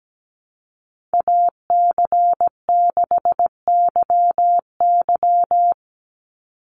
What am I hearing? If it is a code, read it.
Morse code, heard as a keyed tone, about 17 wpm, 700 Hz. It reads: AC6YY
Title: Call Signs - 17wpm